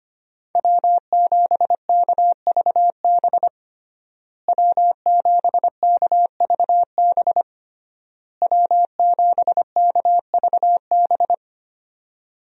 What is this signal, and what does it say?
Morse code, heard as a keyed tone, about 25 wpm, 700 Hz. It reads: W7KVB W7KVB W7KVB
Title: USA Call Signs - 5 Characters - Repeated 3x - 25wpm